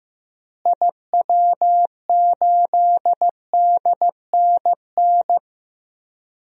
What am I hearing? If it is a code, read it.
Morse code, heard as a keyed tone, about 15 wpm, 700 Hz. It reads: IW8DNN
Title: DXCC-100 Call Signs - 6 Characters - Repeated 3x - 15wpm